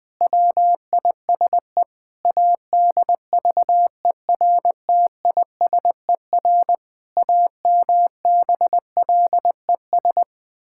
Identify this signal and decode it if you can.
Morse code, heard as a keyed tone, about 20 wpm, 700 Hz. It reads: WISE ADVERTISER AMBLES
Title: Pangram Challenge v2 - 20wpm